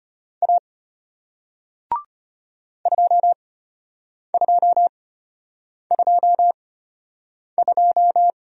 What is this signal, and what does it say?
Morse code, heard as a keyed tone, about 33 wpm, spaced as for 8 wpm, 700 Hz. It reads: A 2222